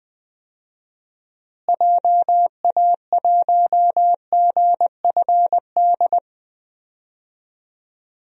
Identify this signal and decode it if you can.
Morse code, heard as a keyed tone, about 21 wpm, 700 Hz. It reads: JA1GFD
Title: DXCC-100 Call Signs - 6 Characters - Repeated 3x - 20wpm